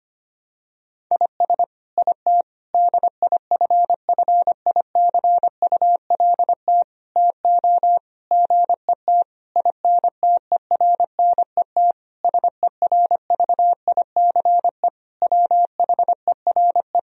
Morse code, heard as a keyed tone, about 25 wpm, 700 Hz. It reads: IS IT DIFFICULT TO GET INTERNET SERVICE WHERE
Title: Sentences from Top 1000 Words - Review Part 4 - 25wpm